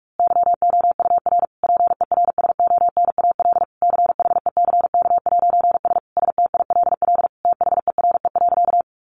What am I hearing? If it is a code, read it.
Morse code, heard as a keyed tone, about 44 wpm, 700 Hz. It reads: YOUR PERSONAL CHECK'S STILL THERE.
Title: A Sound Of Thunder 45wpm